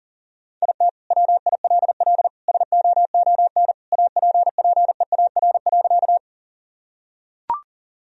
Morse code, heard as a keyed tone, about 40 wpm, 700 Hz. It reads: IT WILL SOON APPEAR.
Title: Sentences from Top 1000 Words - Review Part 2 - 40wpm